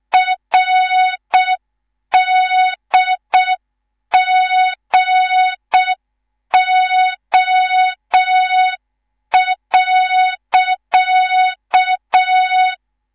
Morse code, heard as a keyed tone, about 6 wpm, 3700 Hz. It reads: RDGO.
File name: Morze-HA2MN / hang / 19 GYAKORLAT